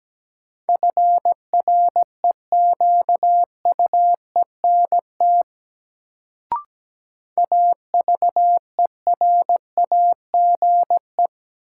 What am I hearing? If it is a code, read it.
Morse code, heard as a keyed tone, about 17 wpm, 700 Hz. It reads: FREQUENT AVERAGE
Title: Sentences from Top 1400 Words - New Word Drill Alt - 17wpm